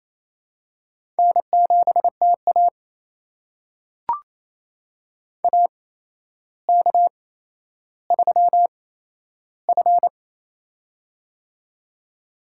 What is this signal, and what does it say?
Morse code, heard as a keyed tone, about 27 wpm, spaced as for 12 wpm, 700 Hz. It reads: N7TA AK3F
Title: USA Call Signs - 4 Characters - 8x Character Spacing - 28wpm